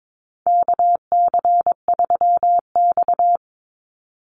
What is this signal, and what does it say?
Morse code, heard as a keyed tone, about 22 wpm, 700 Hz. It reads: KC3X
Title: Call Signs Repeated 2x - 22wpm